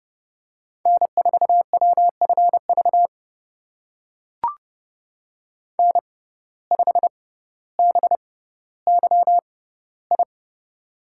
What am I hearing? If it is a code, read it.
Morse code, heard as a keyed tone, about 30 wpm, spaced as for 15 wpm, 700 Hz. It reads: N4WFV N5BYI
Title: USA Call Signs - 5 Characters - 6x Character Spacing - 30wpm